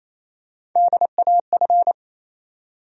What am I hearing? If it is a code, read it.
Morse code, heard as a keyed tone, about 27 wpm, 700 Hz. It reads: DAF
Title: USA Call Sign Postfixes - 3 Letters 28wpm